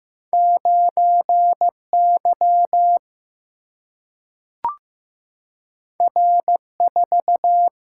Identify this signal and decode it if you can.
Morse code, heard as a keyed tone, about 15 wpm, 700 Hz. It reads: 9Y R4
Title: Single-Letter Single-Number 15wpm